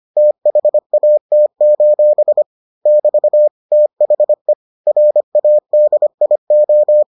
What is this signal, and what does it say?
Morse code, heard as a keyed tone, about 25 wpm, 600 Hz. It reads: THAT: <BT> THE RADIO